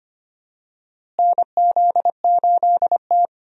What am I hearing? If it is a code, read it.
Morse code, heard as a keyed tone, about 25 wpm, 700 Hz. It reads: NZ8T
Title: USA Call Signs - 4 Characters - Repeated 3x - 25wpm